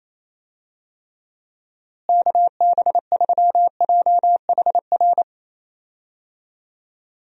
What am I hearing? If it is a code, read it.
Morse code, heard as a keyed tone, about 28 wpm, 700 Hz. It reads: KB3JHR
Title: USA Call Signs - 6 Characters - Repeated 2x - 28wpm